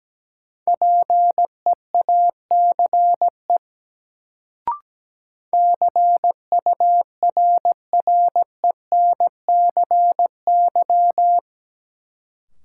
Morse code, heard as a keyed tone, about 17 wpm, 700 Hz. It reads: PEACE CURRENCY
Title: Sentences from Top 1500 Words - New Word Drill - 17wpm